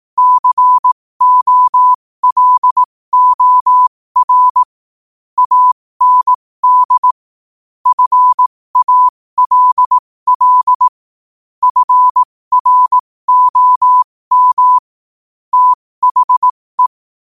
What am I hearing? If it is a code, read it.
Morse code, heard as a keyed tone, about 18 wpm, spaced as for 15 wpm, 1000 Hz. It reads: COLOR AND FALL FROM THE
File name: qotdfiles / 15wpm